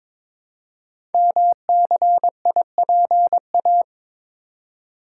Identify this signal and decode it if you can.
Morse code, heard as a keyed tone, about 23 wpm, 700 Hz. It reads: MCIPA
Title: Code Groups - 5 Letters - 22wpm